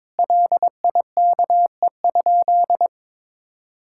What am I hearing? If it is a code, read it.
Morse code, heard as a keyed tone, about 22 wpm, 700 Hz. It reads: LIKE?
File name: practice / 22wpm / Sentences with 3-Word Phrases - 22wpm